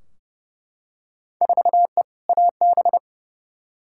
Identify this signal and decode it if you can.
Morse code, heard as a keyed tone, about 30 wpm, 700 Hz. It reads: 4E AB
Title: Field Day Exchanges Encore - 30wpm